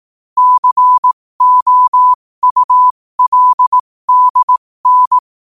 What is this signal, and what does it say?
Morse code, heard as a keyed tone, about 18 wpm, spaced as for 15 wpm, 1000 Hz. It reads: COULDN
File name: qotdfiles / 15wpm